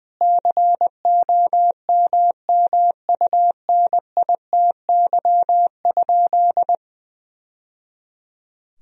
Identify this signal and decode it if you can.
Morse code, heard as a keyed tone, about 21 wpm, 700 Hz. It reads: COMMUNITY?
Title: Sentences from Top 1000 Words - Review Part 4 - 20wpm